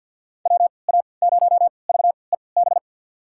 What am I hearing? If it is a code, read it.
Morse code, heard as a keyed tone, about 53 wpm, spaced as for 29 wpm, 700 Hz. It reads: WA0VEB
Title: USA Call Signs - 3x Character Spacing - 50wpm